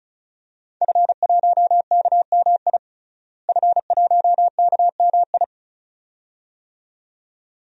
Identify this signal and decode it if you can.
Morse code, heard as a keyed tone, about 36 wpm, 700 Hz. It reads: F1KMI F1KMI
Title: DXCC-100 Call Signs - 5 Characters - Repeated 2x - 35wpm